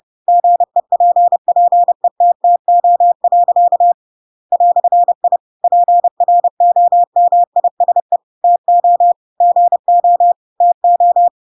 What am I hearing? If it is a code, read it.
Morse code, heard as a keyed tone, about 30 wpm, 700 Hz. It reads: GEPPETTO. "I PROMISE TO GO TO
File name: Pinocchio / Pinocchio ch8 30 30 ik1hge